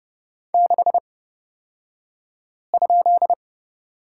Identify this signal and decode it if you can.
Morse code, heard as a keyed tone, about 30 wpm, spaced as for 5 wpm, 700 Hz. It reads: 6?
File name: Lesson 38d - Letter q - Two Characters 30wpm